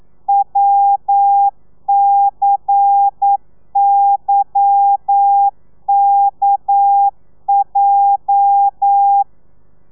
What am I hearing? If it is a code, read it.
Morse code, heard as a keyed tone, about 9 wpm, 800 Hz. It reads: WCYKJ